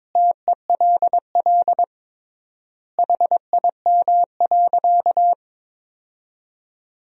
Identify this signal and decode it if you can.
Morse code, heard as a keyed tone, about 22 wpm, 700 Hz. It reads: TELL HIM.
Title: 3x Word-Spacing - Sentences from Top 200 Words - 22wpm